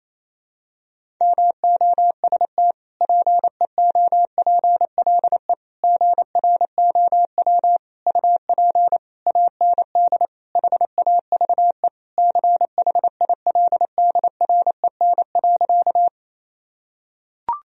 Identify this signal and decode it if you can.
Morse code, heard as a keyed tone, about 28 wpm, 700 Hz. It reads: MOST PEOPLE GROW UP AND HAVE CHILDREN.
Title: Sentences from Top 1000 Words - Review Part 1 - 28wpm